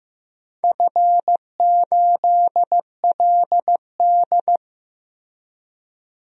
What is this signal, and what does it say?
Morse code, heard as a keyed tone, about 15 wpm, 700 Hz. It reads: F8LD
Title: DXCC-100 Call Signs - 4 Characters - Repeated 3x - 15wpm